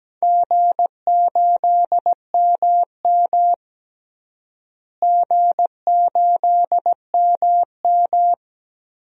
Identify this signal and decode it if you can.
Morse code, heard as a keyed tone, about 17 wpm, 700 Hz. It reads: G8MM G8MM
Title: DXCC-100 Call Signs - 4 Characters - Repeated 2x - 17wpm